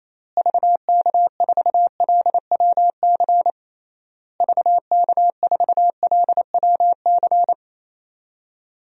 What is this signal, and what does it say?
Morse code, heard as a keyed tone, about 28 wpm, 700 Hz. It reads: VK4LWC VK4LWC
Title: DXCC-100 Call Signs - 6 Characters - Repeated 3x - 28wpm